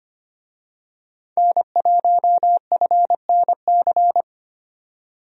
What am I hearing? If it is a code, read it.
Morse code, heard as a keyed tone, about 25 wpm, 700 Hz. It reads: N1FNC
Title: USA Call Signs - 5 Characters - 25wpm